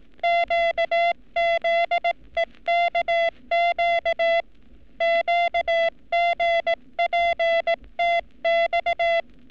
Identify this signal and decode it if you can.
Morse code, heard as a keyed tone, about 18 wpm, 680 Hz. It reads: QZEKQ QGPTX
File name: training / NAVPERS 11224-11235 / NAVPERS 11233 RA RB